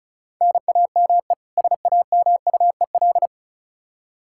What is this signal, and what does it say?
Morse code, heard as a keyed tone, about 36 wpm, 700 Hz. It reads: NAME SAMUEL
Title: Practice QSO - Exchange 1 - Beginning - 35wpm